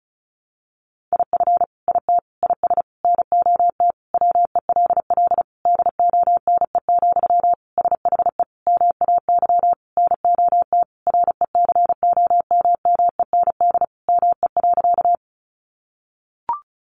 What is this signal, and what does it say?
Morse code, heard as a keyed tone, about 34 wpm, 700 Hz. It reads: IF IT IS NOT WELL DONE, SHE MAY NOT RECOMMEND ME.
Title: Sentences from Top 1200 Words 35wpm